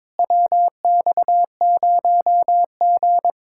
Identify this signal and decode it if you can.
Morse code, heard as a keyed tone, about 23 wpm, 700 Hz. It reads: WX0G
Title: Speed-Racing Common QSO Phrases Encore - 22wpm